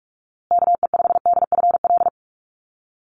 Morse code, heard as a keyed tone, about 44 wpm, 700 Hz. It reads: KE5DFL
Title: USA Call Signs Repeated 3x 45wpm